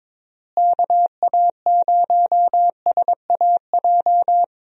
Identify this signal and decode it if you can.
Morse code, heard as a keyed tone, about 22 wpm, 700 Hz. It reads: KA0SAJ
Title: USA Call Signs - 6 Characters - Repeated 3x - 22wpm